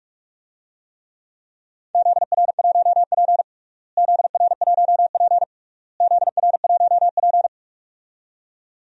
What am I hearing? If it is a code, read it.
Morse code, heard as a keyed tone, about 46 wpm, 690 Hz. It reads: ZR1P ZR1P ZR1P
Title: DXCC-100 Call Signs - 4 Characters - Repeated 3x - 45wpm